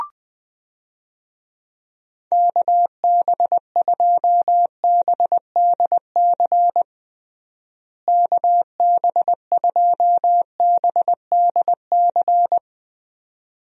Morse code, heard as a keyed tone, about 19 wpm, 700 Hz. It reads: KB2BDC KB2BDC
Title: USA Call Signs - 6 Characters - Repeated 2x - 20wpm